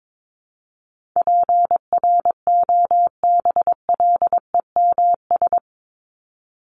Morse code, heard as a keyed tone, about 22 wpm, 700 Hz. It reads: PROBLEMS
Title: ES:S Suffix - Top 100 Words with Base Words 22wpm